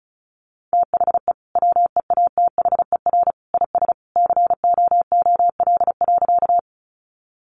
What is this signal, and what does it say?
Morse code, heard as a keyed tone, about 34 wpm, 700 Hz. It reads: THE WEATHER IS COOL.
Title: Sentences from Top 1200 Words 35wpm